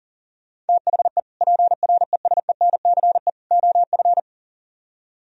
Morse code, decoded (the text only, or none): THE PRESENCE OF